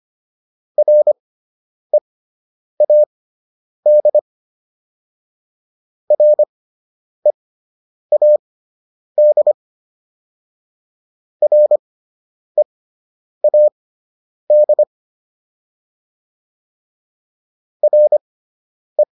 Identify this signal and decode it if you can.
Morse code, heard as a keyed tone, about 25 wpm, spaced as for 9 wpm, 600 Hz. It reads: READ READ READ RE